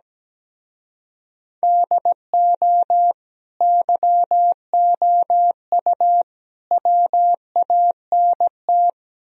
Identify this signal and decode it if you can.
Morse code, heard as a keyed tone, about 17 wpm, 700 Hz. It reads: DO YOU WANT
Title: Speed-Racing - Sets of 3 Words 17wpm